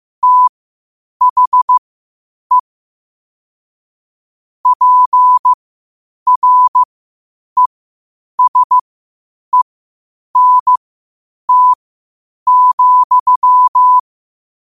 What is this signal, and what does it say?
Morse code, heard as a keyed tone, about 15 wpm, spaced as for 8 wpm, 1000 Hz. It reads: THE PRESENT,
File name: qotdfiles / 8wpm